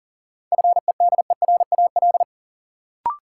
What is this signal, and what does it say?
Morse code, heard as a keyed tone, about 40 wpm, 700 Hz. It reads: FEDERAL
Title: Sets of 3 Words - New Word Drill - 40wpm